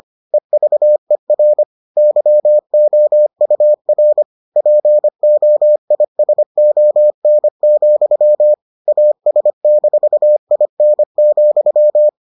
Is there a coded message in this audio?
Yes: Morse code from 0.3 to 12.1 s, heard as a keyed tone, about 25 wpm, 600 Hz. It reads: EVER YOUR POISON, AS-IN,